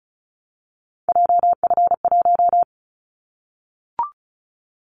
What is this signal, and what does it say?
Morse code, heard as a keyed tone, about 36 wpm, 690 Hz. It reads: JF1
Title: Callsign Prefixes for Top 25 DXCC - 35wpm